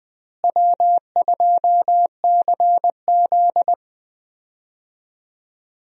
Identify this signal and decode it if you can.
Morse code, heard as a keyed tone, about 20 wpm, 700 Hz. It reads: W2CZ
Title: USA Call Signs - 4 Characters - Repeated 2x - 20wpm